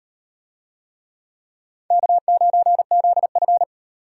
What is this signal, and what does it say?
Morse code, heard as a keyed tone, about 36 wpm, 700 Hz. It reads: K9ZF